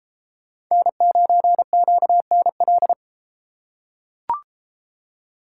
Morse code, heard as a keyed tone, about 31 wpm, 700 Hz. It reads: N9QNL